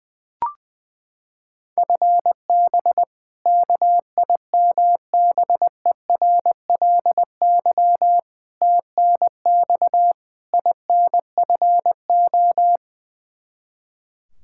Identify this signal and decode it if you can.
Morse code, heard as a keyed tone, about 20 wpm, 700 Hz. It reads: FB KIMBERLY TNX INFO
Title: Practice QSO - Exchange 3 - More Info - 20wpm